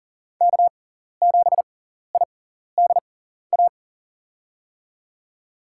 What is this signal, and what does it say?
Morse code, heard as a keyed tone, about 40 wpm, spaced as for 14 wpm, 700 Hz. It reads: K7IDA
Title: USA Call Signs - 5 Characters - 6x Character Spacing - 40wpm